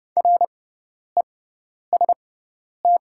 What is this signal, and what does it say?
Morse code, heard as a keyed tone, about 30 wpm, spaced as for 10 wpm, 690 Hz. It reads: REST
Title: 6x Character Spacing - Top 500 Words - 30wpm